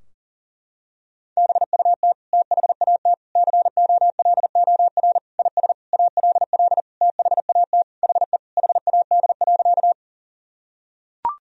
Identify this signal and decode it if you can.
Morse code, heard as a keyed tone, about 37 wpm, 700 Hz. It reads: BUT THAT COLOR IS ALL THAT HE HAD.
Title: Sentences from Top 1000 Words - Review Part 2 - 40wpm